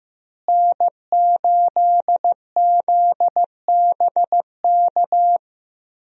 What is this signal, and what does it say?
Morse code, heard as a keyed tone, about 15 wpm, 700 Hz. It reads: N8ZBK
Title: USA Call Signs - 5 Characters - Repeated 3x - 15wpm